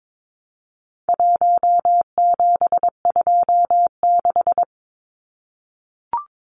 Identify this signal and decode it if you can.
Morse code, heard as a keyed tone, about 22 wpm, 700 Hz. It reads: 1726
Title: Numbers - 4-Digit 22wpm